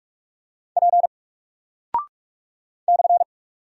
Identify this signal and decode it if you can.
Morse code, heard as a keyed tone, about 45 wpm, 700 Hz. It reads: P /
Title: Single Character (Letter:Number:Punctuation:Prosign) Alt 45wpm